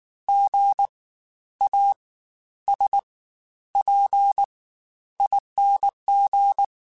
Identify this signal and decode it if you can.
Morse code, heard as a keyed tone, about 19 wpm, spaced as for 11 wpm, 800 Hz. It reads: GASPING
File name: Spaced Alphas / Spaced Alphas mp3s / Gs sp @18